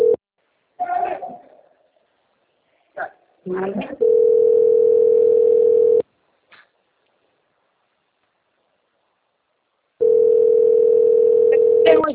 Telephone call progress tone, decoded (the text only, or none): ringback tone